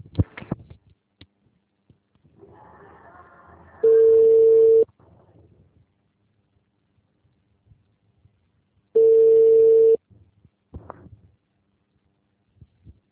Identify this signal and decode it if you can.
Ringback tone (China, 450 Hz).